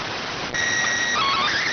SSTV (slow-scan television), PD 290.